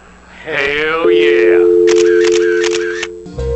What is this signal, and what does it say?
Dial tone (North America), 350 + 440 Hz, with speech.